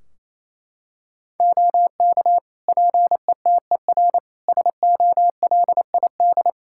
Morse code, heard as a keyed tone, about 28 wpm, 700 Hz. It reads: OK PETER SOLID
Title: K1USN SST QSO Practice 28wpm